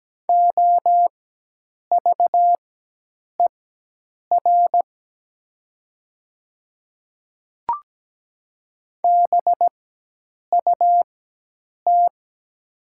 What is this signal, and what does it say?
Morse code, heard as a keyed tone, about 17 wpm, spaced as for 8 wpm, 700 Hz. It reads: OVER BUT
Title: 4x Character Spacing - Top 100 Words - 17wpm